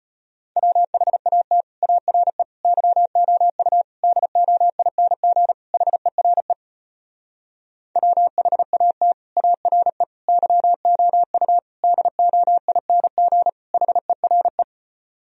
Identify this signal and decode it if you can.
Morse code, heard as a keyed tone, about 33 wpm, 700 Hz. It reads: WHAT ARE YOU DOING HERE WHAT ARE YOU DOING HERE